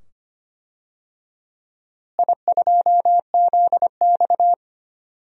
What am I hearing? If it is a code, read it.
Morse code, heard as a keyed tone, about 25 wpm, 700 Hz. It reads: I2ZX
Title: DXCC-100 Call Signs - 4 Characters - Repeated 2x - 25wpm